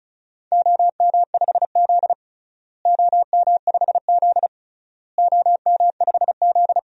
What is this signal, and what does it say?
Morse code, heard as a keyed tone, about 34 wpm, 700 Hz. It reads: OM5Z OM5Z OM5Z
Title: DXCC-100 Call Signs - 4 Characters - Repeated 3x - 35wpm